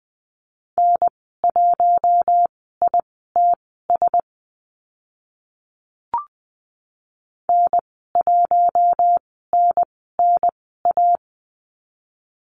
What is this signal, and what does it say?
Morse code, heard as a keyed tone, about 20 wpm, spaced as for 14 wpm, 700 Hz. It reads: N1ITS N1NNA
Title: 048 Lesson 9g - Number 4 - Call Signs 20wpm